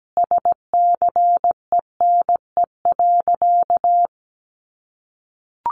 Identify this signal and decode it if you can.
Morse code, heard as a keyed tone, about 17 wpm, 700 Hz. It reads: SCENE.
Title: Sentences from Top 1100 Words 17wpm